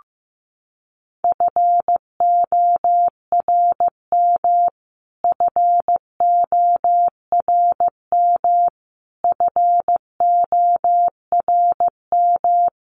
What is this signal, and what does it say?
Morse code, heard as a keyed tone, about 15 wpm, 700 Hz. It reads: FORM FORM FORM
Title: Top 200 Words Repeated 15wpm